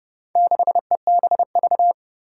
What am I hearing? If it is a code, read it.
Morse code, heard as a keyed tone, about 29 wpm, 700 Hz. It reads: 6EBV
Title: Four Challenging Letter-Number Combinations 30wpm